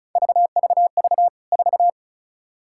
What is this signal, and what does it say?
Morse code, heard as a keyed tone, about 36 wpm, 700 Hz. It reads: VVV 4
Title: K1USN SST QSO Practice 35wpm